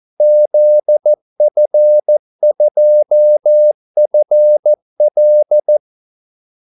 Morse code, heard as a keyed tone, about 14 wpm, 600 Hz. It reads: ZF2FL
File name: Callsigns-14wpm